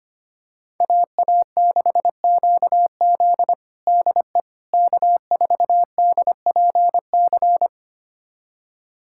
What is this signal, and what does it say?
Morse code, heard as a keyed tone, about 25 wpm, 700 Hz. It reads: AA6QZ DE K4DPC
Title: USA Callsign DE Callsign - 5 Characters - 25wpm